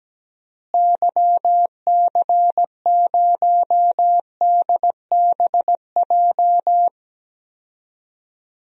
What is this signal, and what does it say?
Morse code, heard as a keyed tone, about 17 wpm, 700 Hz. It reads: YC0DBJ
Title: DXCC-100 Call Signs - 6 Characters - Repeated 2x - 17wpm